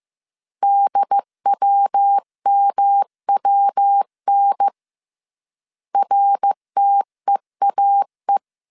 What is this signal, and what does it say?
Morse code, heard as a keyed tone, about 15 wpm, 800 Hz. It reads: DWMWN RTEAE
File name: CW / DET / 1 3 1500C S C 14wpm 2458